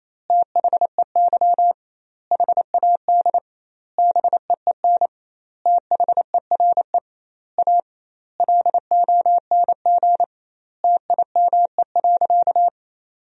Morse code, heard as a keyed tone, about 28 wpm, 700 Hz. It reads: THEY HAD BEEN THERE A LONG TIME.
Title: 2x Word-Spacing - Sentences from Top 400 Words - 28wpm